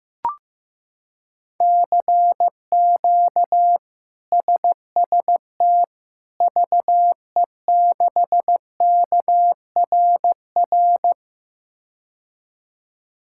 Morse code, heard as a keyed tone, about 15 wpm, 700 Hz. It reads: CQ SST VE6KRR
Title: K1USN SST QSO Practice - 15wpm